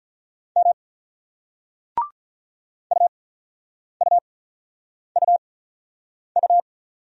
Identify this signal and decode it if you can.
Morse code, heard as a keyed tone, about 46 wpm, spaced as for 9 wpm, 700 Hz. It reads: M UUUU